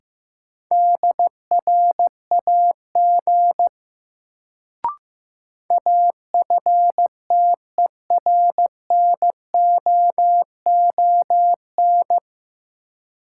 Morse code, heard as a keyed tone, about 15 wpm, 700 Hz. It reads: DRAG AFTERNOON
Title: Sentences from Top 1500 Words - New Word Drill - 15wpm